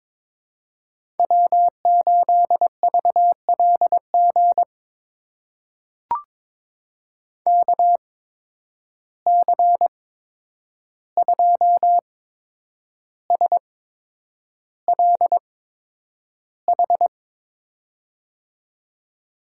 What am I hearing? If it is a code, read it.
Morse code, heard as a keyed tone, about 22 wpm, spaced as for 9 wpm, 700 Hz. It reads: W8VLG KC2SLH
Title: USA Call Signs - 8x Character Spacing - 22wpm